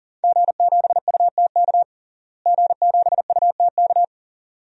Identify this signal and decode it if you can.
Morse code, heard as a keyed tone, about 40 wpm, 700 Hz. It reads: G7UTK G7UTK
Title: DXCC-100 Call Signs - 5 Characters - Repeated 2x - 40wpm